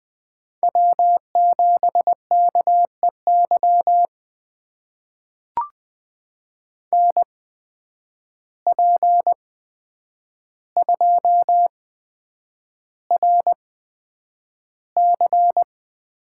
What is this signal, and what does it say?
Morse code, heard as a keyed tone, about 20 wpm, spaced as for 9 wpm, 700 Hz. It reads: W7KEY NP2RC
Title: USA Call Signs - 5 Characters - 8x Character Spacing - 20wpm